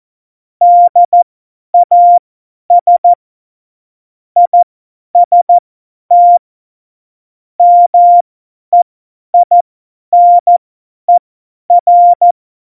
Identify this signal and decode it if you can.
Morse code, heard as a keyed tone, about 14 wpm, spaced as for 10 wpm, 700 Hz. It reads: DAS IST MEINER
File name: Morsekurs / Witze / Witz 22. Januar 2019, 50 F70